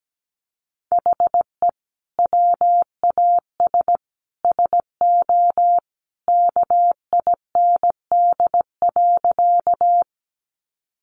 Morse code, heard as a keyed tone, about 17 wpm, 700 Hz. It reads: HE WAS SO KIND.
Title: Sentences from Top 200 Words 17wpm